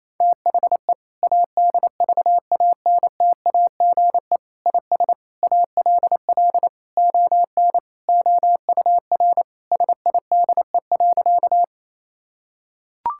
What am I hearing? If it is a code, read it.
Morse code, heard as a keyed tone, about 28 wpm, 700 Hz. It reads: THE ADVANTAGE IS ALL ON OUR SIDE.
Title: Sentences from Top 1000 Words - Review Part 6 - 28wpm